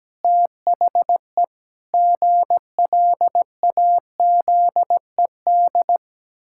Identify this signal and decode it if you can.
Morse code, heard as a keyed tone, about 17 wpm, 700 Hz. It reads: THE GLAZED
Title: Pangram Challenge v2 - 17wpm